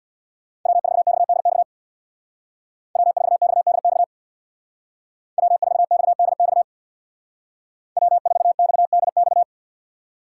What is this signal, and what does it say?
Morse code, heard as a keyed tone, about 63 wpm, 700 Hz. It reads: W4XDX W4XDX W4XDX W4XDX